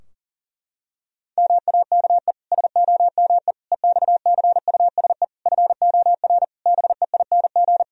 Morse code, heard as a keyed tone, about 38 wpm, 700 Hz. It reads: MAKE SOME EXCUSE FOR BEING
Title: Sentences from Top 1000 Words - Review Part 3 - 40wpm